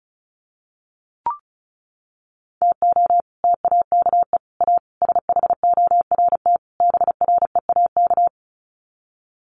Morse code, heard as a keyed tone, about 34 wpm, 700 Hz. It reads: TO TAKE A SHORT BREAK
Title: Sets of 5 Words Encore2 35wpm